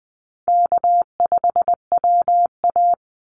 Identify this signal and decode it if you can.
Morse code, heard as a keyed tone, about 20 wpm, 700 Hz. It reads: K5WA
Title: Easy Call Signs Repeated 2x - 20wpm